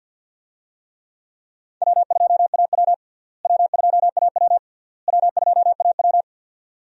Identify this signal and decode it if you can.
Morse code, heard as a keyed tone, about 50 wpm, 700 Hz. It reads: W2AW W2AW W2AW
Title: USA Call Signs - 4 Characters - Repeated 3x - 50wpm